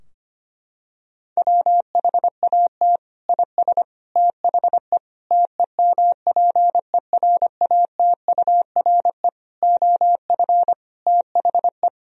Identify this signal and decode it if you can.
Morse code, heard as a keyed tone, about 25 wpm, 700 Hz. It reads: WHAT IS THE TEMPERATURE OF THE
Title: Sentences from Top 1000 Words - Review Part 4 - 25wpm